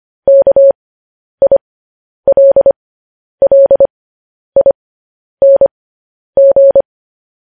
Morse code, heard as a keyed tone, about 25 wpm, spaced as for 10 wpm, 550 Hz. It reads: KILLING